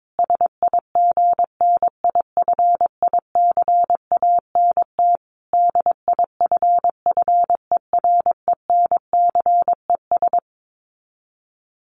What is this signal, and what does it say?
Morse code, heard as a keyed tone, about 22 wpm, 700 Hz. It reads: SIGNIFICANT DIFFERENCES
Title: Sets of 5 Words Encore2 22wpm